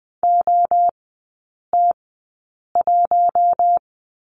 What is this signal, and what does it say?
Morse code, heard as a keyed tone, about 20 wpm, spaced as for 8 wpm, 700 Hz. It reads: OT1
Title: 041 Lesson 8e - Number 1 - Three Characters 20wpm